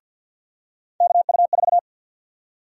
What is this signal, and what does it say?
Morse code, heard as a keyed tone, about 53 wpm, 700 Hz. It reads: KU4
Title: USA Call Sign Prefixes - 50wpm